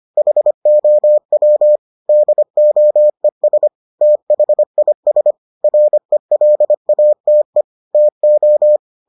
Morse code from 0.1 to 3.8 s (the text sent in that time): HOW DOES